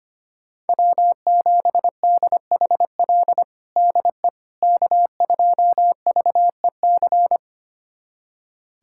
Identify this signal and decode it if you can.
Morse code, heard as a keyed tone, about 25 wpm, 700 Hz. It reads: W7DHL DE K2VEC
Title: USA Callsign DE Callsign - 5 Characters - 25wpm